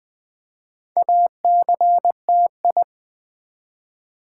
Morse code, heard as a keyed tone, about 20 wpm, 700 Hz. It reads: ACTI
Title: Build A Word - 2 Letters At A Time - 6-Letter Words - 20wpm